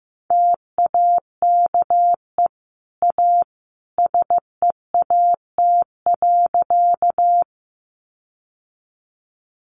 Morse code, heard as a keyed tone, about 15 wpm, 700 Hz. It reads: TAKE A SEAT.
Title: Sentences from Top 1500 Words 15wpm